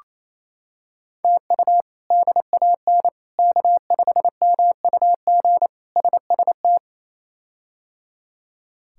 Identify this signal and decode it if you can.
Morse code, heard as a keyed tone, about 28 wpm, 700 Hz. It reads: TU DAN K5MUG SST
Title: K1USN SST QSO Practice - 28wpm